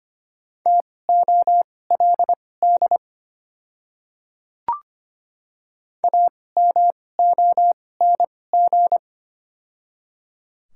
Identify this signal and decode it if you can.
Morse code, heard as a keyed tone, about 25 wpm, spaced as for 18 wpm, 700 Hz. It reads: TOLD AMONG
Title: 2x Character Spacing - Top 500 Words - 25wpm